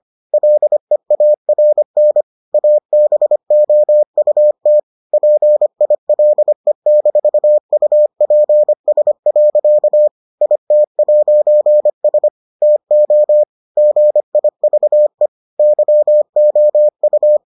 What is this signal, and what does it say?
Morse code, heard as a keyed tone, about 25 wpm, 600 Hz. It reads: LEARN ABOUT PILE-UPS. IT'S TO GIVE YOU